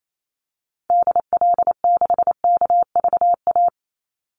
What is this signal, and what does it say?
Morse code, heard as a keyed tone, about 28 wpm, 700 Hz. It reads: DL6KVA
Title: Call Signs Repeated 2x - 28wpm